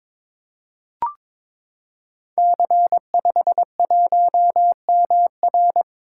Morse code, heard as a keyed tone, about 22 wpm, 700 Hz. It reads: C51MR
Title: Code Groups - 5 Letter:Numbers - 22wpm